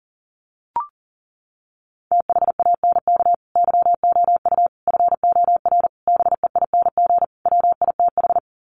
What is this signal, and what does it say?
Morse code, heard as a keyed tone, about 40 wpm, 700 Hz. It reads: THANK YOU FOR BEING WITH
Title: Sets of 5 Words 40wpm